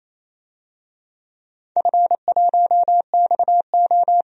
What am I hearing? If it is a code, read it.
Morse code, heard as a keyed tone, about 27 wpm, 700 Hz. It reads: F1XO
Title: DXCC-100 Call Signs - 4 Characters - 28wpm